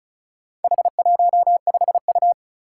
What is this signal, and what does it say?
Morse code, heard as a keyed tone, about 36 wpm, 700 Hz. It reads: H15U
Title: Four Challenging Letter-Number Combinations 35wpm